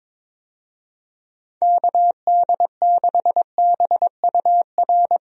Morse code, heard as a keyed tone, about 22 wpm, 700 Hz. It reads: KD6BUR
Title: USA Call Signs - 6 Characters - Repeated 2x - 22wpm